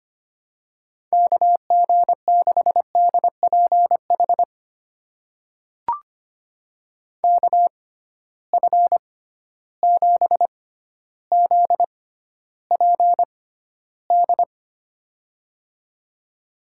Morse code, heard as a keyed tone, about 25 wpm, spaced as for 13 wpm, 700 Hz. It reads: KG6DPH KF7ZPD
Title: USA Call Signs - 6 Characters - 6x Character Spacing - 25wpm